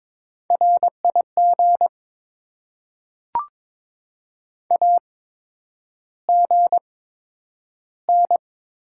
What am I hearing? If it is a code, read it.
Morse code, heard as a keyed tone, about 22 wpm, spaced as for 9 wpm, 700 Hz. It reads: RIG AGN
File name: Top 20 QSO Elements - 8x Character Spacing - 22wpm